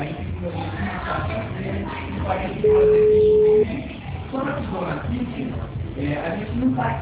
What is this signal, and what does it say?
Ringback tone (Europe, 425 Hz), with speech.